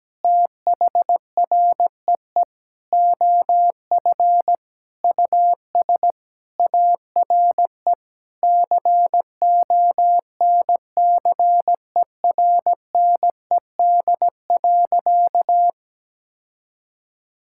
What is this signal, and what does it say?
Morse code, heard as a keyed tone, about 17 wpm, 700 Hz. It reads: THREE OF US ARE CONCERNED.
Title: Sentences from Top 1000 Words - Review Part 6 - 17wpm